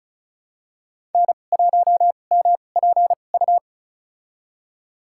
Morse code, heard as a keyed tone, about 36 wpm, spaced as for 25 wpm, 700 Hz. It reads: N1MPU
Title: USA Call Signs - 2x Character Spacing - 35wpm